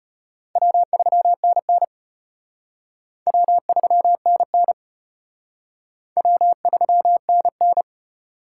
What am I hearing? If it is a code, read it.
Morse code, heard as a keyed tone, about 31 wpm, 700 Hz. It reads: W3NN W3NN W3NN
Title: Speed Racing - USA 1x2 Call Signs 25wpm